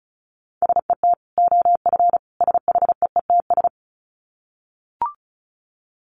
Morse code, heard as a keyed tone, about 34 wpm, 700 Hz. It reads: SET OF SHEETS
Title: Dit Challenge 35wpm